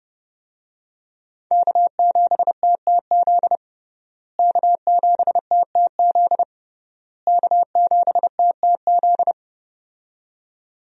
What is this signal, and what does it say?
Morse code, heard as a keyed tone, about 30 wpm, 700 Hz. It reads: K7TTZ K7TTZ K7TTZ
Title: USA Call Signs - 5 Characters - Repeated 3x - 30wpm